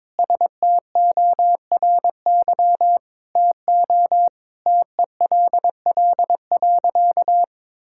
Morse code, heard as a keyed tone, about 22 wpm, 700 Hz. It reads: STORY TO TELL.
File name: practice / 22wpm / Sentences with 2-Word Phrases - 22wpm